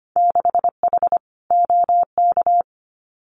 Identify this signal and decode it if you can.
Morse code, heard as a keyed tone, about 25 wpm, 690 Hz. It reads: 6H OK
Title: Winter Field Day Exchanges Encore 25wpm